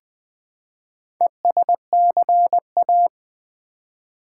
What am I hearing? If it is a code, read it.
Morse code, heard as a keyed tone, about 20 wpm, 700 Hz. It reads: ESCA
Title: Build A Word - 4 Letters At A Time - 8-Letter Words - 20wpm